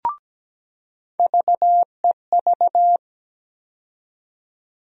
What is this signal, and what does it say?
Morse code, heard as a keyed tone, about 17 wpm, 700 Hz. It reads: VEV
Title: USA Call Sign Postfixes - 3 Letters 17wpm